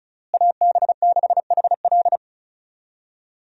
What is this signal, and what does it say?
Morse code, heard as a keyed tone, about 36 wpm, 700 Hz. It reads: AB6HL
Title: USA Call Signs - 5 Characters - Repeated 3x - 35wpm